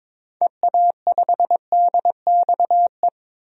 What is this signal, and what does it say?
Morse code, heard as a keyed tone, about 22 wpm, 700 Hz. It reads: EA5DXE
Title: DXCC-100 Call Signs - 6 Characters - 22wpm